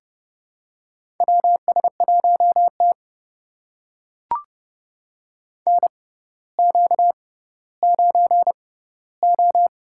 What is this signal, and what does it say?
Morse code, heard as a keyed tone, about 32 wpm, spaced as for 16 wpm, 700 Hz. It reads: WS1T NQ9O
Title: USA Call Signs - 4 Characters - 6x Character Spacing - 30wpm